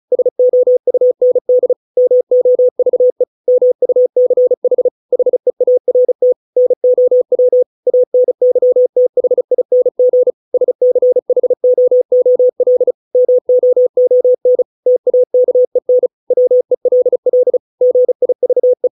Morse code, heard as a keyed tone, about 34 wpm, 500 Hz. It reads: SOUND MOVE MUCH HEART NOW ANYTHING SCHOOL MOON TAKEN WELL GIVE